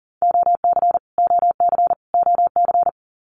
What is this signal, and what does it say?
Morse code, heard as a keyed tone, about 40 wpm, 700 Hz. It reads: OC OC OC
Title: Abbreviations Repeated 3x - 40wpm